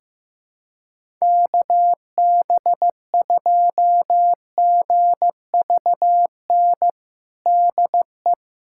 Morse code, heard as a keyed tone, about 15 wpm, 700 Hz. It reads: KB2GVN DE
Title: USA Callsign DE Callsign - 6 Characters - 15wpm